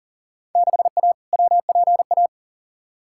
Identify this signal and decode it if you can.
Morse code, heard as a keyed tone, about 40 wpm, 700 Hz. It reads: 6A WPA
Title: Field Day Exchanges Encore - 40wpm